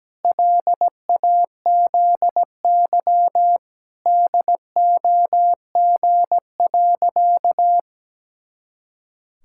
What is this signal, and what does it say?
Morse code, heard as a keyed tone, about 17 wpm, 700 Hz. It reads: LAZY DOG.
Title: Pangram Challenge v2 - 17wpm